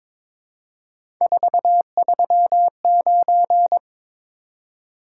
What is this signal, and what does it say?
Morse code, heard as a keyed tone, about 22 wpm, 700 Hz. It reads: 439
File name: practice / 22wpm / Three Letter-Number Combinations - 22wpm